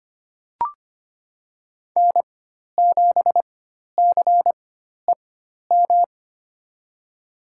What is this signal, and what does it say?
Morse code, heard as a keyed tone, about 25 wpm, spaced as for 12 wpm, 700 Hz. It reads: N7CEM
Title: USA Call Signs - 5 Characters - 4x Character Spacing - 25wpm